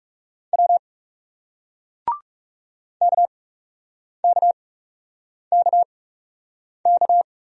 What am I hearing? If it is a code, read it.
Morse code, heard as a keyed tone, about 40 wpm, spaced as for 8 wpm, 700 Hz. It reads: W KKKK